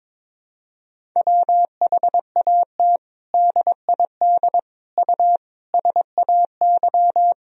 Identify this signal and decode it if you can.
Morse code, heard as a keyed tone, about 21 wpm, 700 Hz. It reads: WHAT DID U SAY
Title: Sentences from Top 200 Words and Abbreviations 22wpm